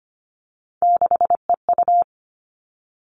Morse code, heard as a keyed tone, about 25 wpm, 700 Hz. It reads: 6EU
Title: All Cut Numbers - 3-Digit 25wpm